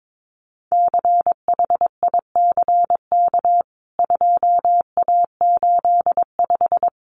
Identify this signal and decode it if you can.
Morse code, heard as a keyed tone, about 22 wpm, 700 Hz. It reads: CHICK 2A85
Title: CWT Exchange 22wpm